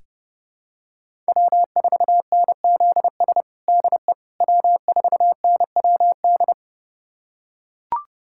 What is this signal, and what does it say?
Morse code, heard as a keyed tone, about 32 wpm, 700 Hz. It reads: W4NZS DE W4NWD
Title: USA Callsign DE Callsign - 5 Characters - 30wpm